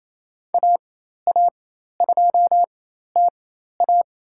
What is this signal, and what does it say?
Morse code, heard as a keyed tone, about 27 wpm, spaced as for 13 wpm, 700 Hz. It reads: AA2TA
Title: USA Call Signs - 5 Characters - 4x Character Spacing - 28wpm